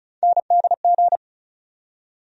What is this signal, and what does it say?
Morse code, heard as a keyed tone, about 36 wpm, 700 Hz. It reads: NDG
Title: USA Call Sign Postfixes - 3 Letters 35wpm